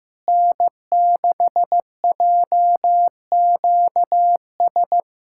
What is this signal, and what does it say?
Morse code, heard as a keyed tone, about 15 wpm, 700 Hz. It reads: N6JQS
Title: USA Callsign DE Callsign - 5 Characters - 15wpm